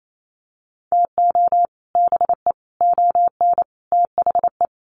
Morse code, heard as a keyed tone, about 28 wpm, 700 Hz. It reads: TO BE ON THE
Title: Sets of 4 Words 28wpm